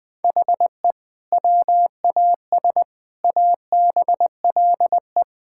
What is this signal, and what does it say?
Morse code, heard as a keyed tone, about 20 wpm, 700 Hz. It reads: HE WAS ABLE
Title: Sentences with 3-Word Phrases - 20wpm